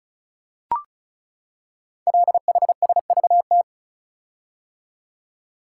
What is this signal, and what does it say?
Morse code, heard as a keyed tone, about 36 wpm, 690 Hz. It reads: LHSVT
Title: Code Groups - 5 Letters - 35wpm